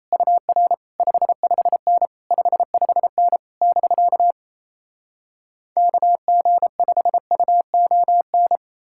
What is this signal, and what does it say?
Morse code, heard as a keyed tone, about 38 wpm, 700 Hz. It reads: UR 55N 55N <BK> KG5UON